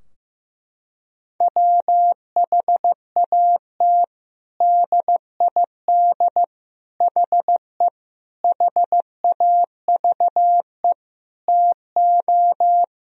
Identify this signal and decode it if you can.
Morse code, heard as a keyed tone, about 15 wpm, 700 Hz. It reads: WHAT DID HE HAVE TO
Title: Sentences from Top 1000 Words - Review Part 4 - 15wpm